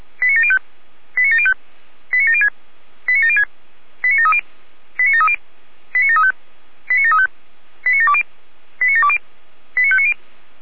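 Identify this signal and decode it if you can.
Five-tone selective call (ZVEI), heard as quick runs of five tones, tones of about 80 ms. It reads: 89705 89705 89706 89706 89730 89730 89735 89735 89720 89720 89590
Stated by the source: ZVEI five-tone selective call received by SDR.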